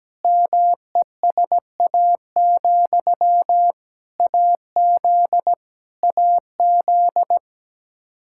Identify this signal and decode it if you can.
Morse code, heard as a keyed tone, about 17 wpm, 700 Hz. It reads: MESA, AZ AZ
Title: Practice QSO - Exchange 1 - Beginning - 17wpm